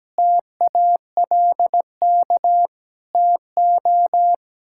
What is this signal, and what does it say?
Morse code, heard as a keyed tone, about 17 wpm, 700 Hz. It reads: TALK TO
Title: Sentences from Top 1600 Words 17wpm